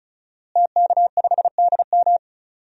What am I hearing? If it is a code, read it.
Morse code, heard as a keyed tone, about 36 wpm, 700 Hz. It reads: TK5DM
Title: DXCC-100 Call Signs - 5 Characters - Repeated 2x - 35wpm